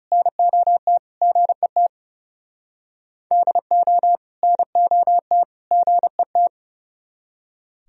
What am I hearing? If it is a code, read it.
Morse code, heard as a keyed tone, about 38 wpm, 700 Hz. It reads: NOT GET DO NOT GET